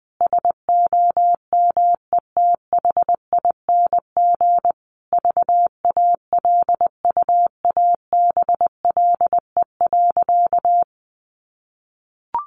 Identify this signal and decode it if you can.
Morse code, heard as a keyed tone, about 20 wpm, 700 Hz. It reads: SOMETHING VALUABLE.
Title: Sentences from Top 1400 Words 20wpm